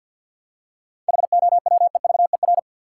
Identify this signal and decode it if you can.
Morse code, heard as a keyed tone, about 50 wpm, 700 Hz. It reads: HOWEVER
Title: Seven Letter Words - 50wpm